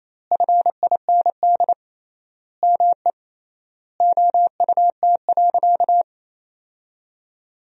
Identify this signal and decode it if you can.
Morse code, heard as a keyed tone, about 28 wpm, 700 Hz. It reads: FIND ME OUT.
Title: 3x Word-Spacing - Sentences from Top 500 Words - 28wpm